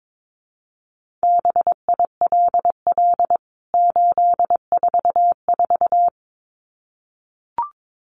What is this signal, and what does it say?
Morse code, heard as a keyed tone, about 22 wpm, 700 Hz. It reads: BILL 844
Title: CWT Exchange 22wpm